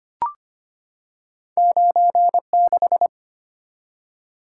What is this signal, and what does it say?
Morse code, heard as a keyed tone, about 25 wpm, 700 Hz. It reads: 96
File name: Two Letter-Number Combinations - 25wpm